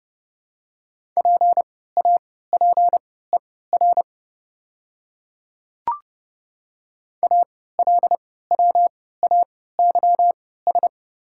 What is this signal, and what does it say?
Morse code, heard as a keyed tone, about 30 wpm, spaced as for 17 wpm, 700 Hz. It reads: PAPER ALWAYS
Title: 3x Character Spacing - Top 300 Words - 30wpm